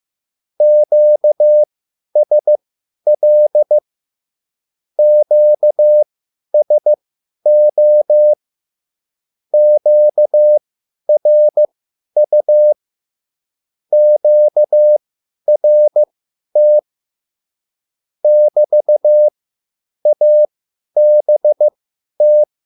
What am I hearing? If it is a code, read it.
Morse code, heard as a keyed tone, about 15 wpm, spaced as for 10 wpm, 600 Hz. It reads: QSL QSO QRU QRT <BT> ABT
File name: QSL / CW lesson